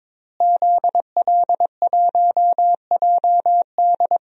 Morse code, heard as a keyed tone, about 22 wpm, 700 Hz. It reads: ZL1JD
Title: DXCC-100 Call Signs - 5 Characters - Repeated 3x - 22wpm